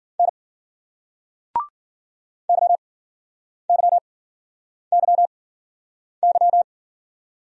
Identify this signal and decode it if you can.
Morse code, heard as a keyed tone, about 51 wpm, spaced as for 9 wpm, 700 Hz. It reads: N YYYY